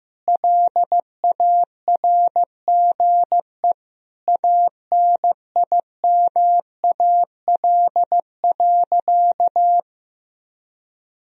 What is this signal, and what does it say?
Morse code, heard as a keyed tone, about 15 wpm, 700 Hz. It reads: LARGE ANIMAL.
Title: Sentences from Top 1000 Words - Review Part 1 - 15wpm